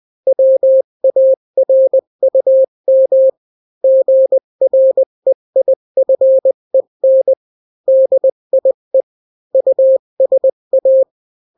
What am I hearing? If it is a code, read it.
Morse code, heard as a keyed tone, about 20 wpm, 530 Hz. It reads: WARUM GREIFEN DIE USA
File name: daily / 18wpm